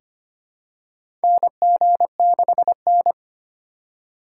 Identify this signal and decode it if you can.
Morse code, heard as a keyed tone, about 25 wpm, 700 Hz. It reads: NG6N
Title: USA Call Signs - 4 Characters - 4x Character Spacing - 25wpm